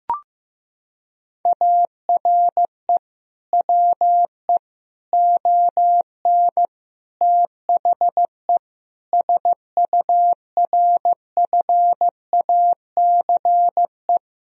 Morse code, heard as a keyed tone, about 14 wpm, 700 Hz. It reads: ARE WE ON THE SURFACE
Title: Sentences from Top 1000 Words - Review Part 3 - 15wpm